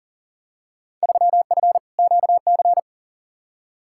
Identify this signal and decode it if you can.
Morse code, heard as a keyed tone, about 40 wpm, 700 Hz. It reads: 3F QC
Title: Field Day Exchanges - 40wpm